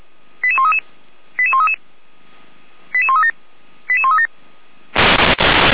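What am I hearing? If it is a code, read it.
Five-tone selective call (ZVEI), heard as quick runs of five tones, tones of about 70 ms. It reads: 88133 88133 88137 88137
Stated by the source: ZVEI five-tone selective call 88402, received on an SDR.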